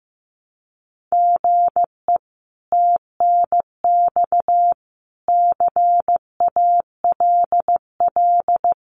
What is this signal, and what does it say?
Morse code, heard as a keyed tone, about 15 wpm, 700 Hz. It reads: GE TNX CALL
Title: Common QSO Phrases 15wpm